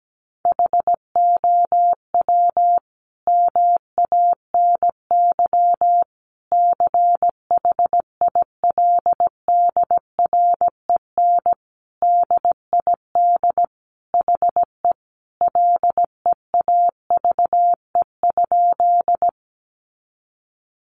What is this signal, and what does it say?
Morse code, heard as a keyed tone, about 17 wpm, 700 Hz. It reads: HOW MANY CHILDREN DID HE LEAVE?
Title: Sentences from Top 400 Words 17wpm